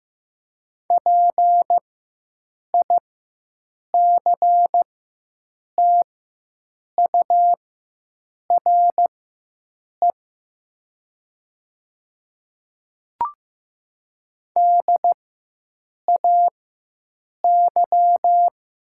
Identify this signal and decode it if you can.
Morse code, heard as a keyed tone, about 15 wpm, spaced as for 7 wpm, 700 Hz. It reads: PICTURE DAY
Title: 4x Character Spacing - Top 200 Words - 15wpm